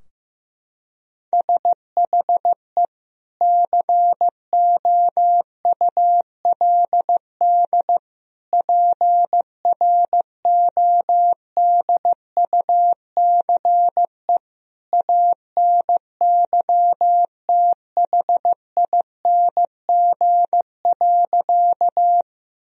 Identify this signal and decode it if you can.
Morse code, heard as a keyed tone, about 15 wpm, 700 Hz. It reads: SHE COULD PRODUCE ANYTHING.
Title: Sentences from Top 1000 Words - Review Part 3 - 15wpm